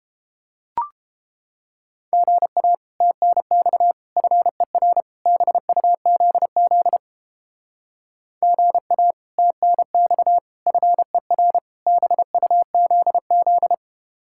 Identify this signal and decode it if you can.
Morse code, heard as a keyed tone, about 31 wpm, 700 Hz. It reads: GA TNX FER BUZZ GA TNX FER BUZZ